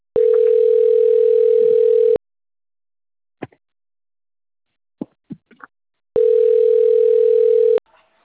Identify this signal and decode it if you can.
Ringback tone (North America), 440 + 480 Hz.
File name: out- - - - - .96767